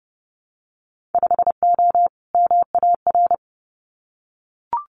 Morse code, heard as a keyed tone, about 30 wpm, 700 Hz. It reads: 5O MAR
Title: Winter Field Day Exchanges Encore 30wpm